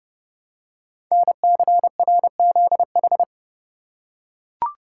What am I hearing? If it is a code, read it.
Morse code, heard as a keyed tone, about 30 wpm, 700 Hz. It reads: NCRZH
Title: Code Groups - 5 Letters - 30wpm